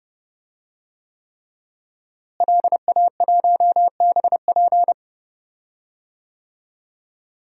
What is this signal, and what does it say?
Morse code, heard as a keyed tone, about 30 wpm, 700 Hz. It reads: LA1BP
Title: DXCC-100 Call Signs - 5 Characters - Repeated 3x - 30wpm